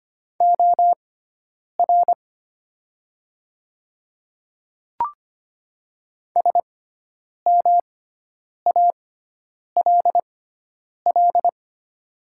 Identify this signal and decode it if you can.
Morse code, heard as a keyed tone, about 25 wpm, spaced as for 9 wpm, 700 Hz. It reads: OR SMALL
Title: 6x Character Spacing - Top 300 Words - 25wpm